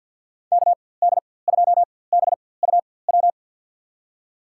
Morse code, heard as a keyed tone, about 52 wpm, spaced as for 23 wpm, 700 Hz. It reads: KD2BUW
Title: USA Call Signs - 6 Characters - 4x Character Spacing - 50wpm